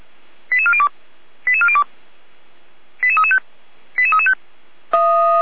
Five-tone selective call (ZVEI), heard as quick runs of five tones, tones of about 70 ms. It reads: 88402 88402 88335 88335
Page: SDR/ZVEI